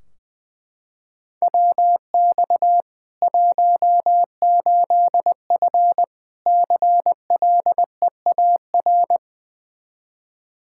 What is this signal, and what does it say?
Morse code, heard as a keyed tone, about 20 wpm, 700 Hz. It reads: WX 18F CLEAR
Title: Practice QSO - Exchange 2 - Info - 20wpm